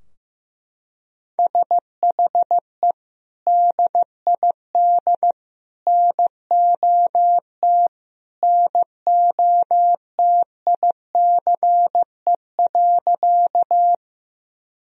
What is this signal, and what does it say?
Morse code, heard as a keyed tone, about 15 wpm, 700 Hz. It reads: SHE DID NOT NOTICE.
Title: Sentences from Top 1000 Words - Review Part 2 - 15wpm